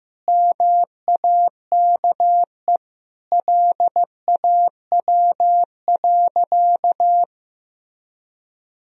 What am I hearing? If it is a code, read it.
Morse code, heard as a keyed tone, about 15 wpm, 700 Hz. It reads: MAKE LAW.
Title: Sentences from Top 1000 Words - Review Part 3 - 15wpm